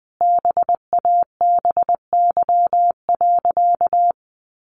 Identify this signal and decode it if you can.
Morse code, heard as a keyed tone, about 20 wpm, 700 Hz. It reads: BABY.
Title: Sentences from Top 1500 Words 20wpm